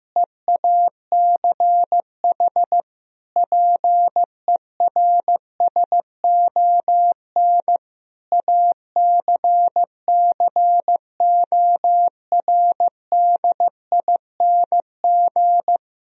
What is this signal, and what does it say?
Morse code, heard as a keyed tone, about 15 wpm, 700 Hz. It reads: EACH PERSON ACCORDING
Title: Sentences from Top 1000 Words - Review Part 4 - 15wpm